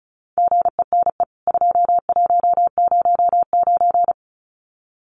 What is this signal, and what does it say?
Morse code, heard as a keyed tone, about 34 wpm, 700 Hz. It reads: GENE 2109
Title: CWT Exchange 35wpm